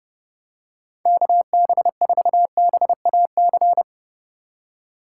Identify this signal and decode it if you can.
Morse code, heard as a keyed tone, about 30 wpm, 700 Hz. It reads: KB4BAC
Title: USA Call Signs - 6 Characters - 2x Character Spacing - 30wpm